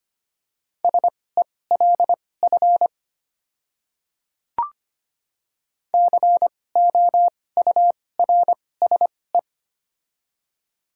Morse code, heard as a keyed tone, about 25 wpm, spaced as for 18 wpm, 700 Hz. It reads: SELF COURSE
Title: 2x Character Spacing - Top 500 Words - 25wpm